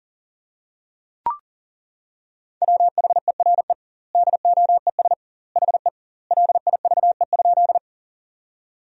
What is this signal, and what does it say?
Morse code, heard as a keyed tone, about 40 wpm, 700 Hz. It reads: WHERE DOES HE LIVE?
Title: 2x Word-Spacing - Sentences from Top 200 Words - 40wpm